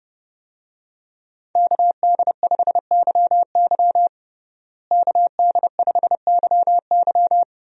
Morse code, heard as a keyed tone, about 30 wpm, 700 Hz. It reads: KD5YY KD5YY
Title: USA Call Signs - 5 Characters - Repeated 2x - 30wpm